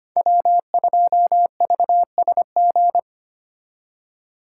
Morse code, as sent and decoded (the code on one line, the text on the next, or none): .-- ..--- ...- ... --.
W2VSG